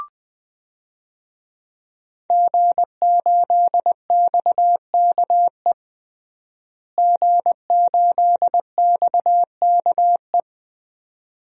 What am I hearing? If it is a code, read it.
Morse code, heard as a keyed tone, about 21 wpm, 700 Hz. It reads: G8XKE G8XKE
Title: DXCC-100 Call Signs - 5 Characters - Repeated 2x - 20wpm